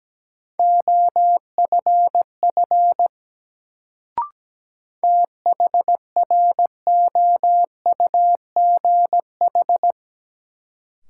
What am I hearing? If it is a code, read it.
Morse code, heard as a keyed tone, about 17 wpm, 700 Hz. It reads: OFF THROUGH
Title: Sentences from Top 200 Words - New Word Drill - 17wpm